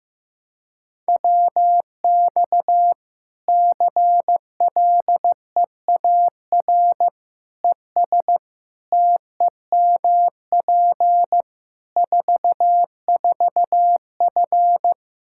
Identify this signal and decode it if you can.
Morse code, heard as a keyed tone, about 15 wpm, 700 Hz. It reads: WX CLEAR ES TEMP 44F
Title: Practice QSO - Exchange 2 - Info - 15wpm